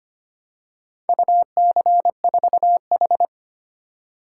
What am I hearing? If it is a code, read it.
Morse code, heard as a keyed tone, about 25 wpm, 700 Hz. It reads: UC4H
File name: practice / 25wpm / Four Letter-Number Combinations - 25wpm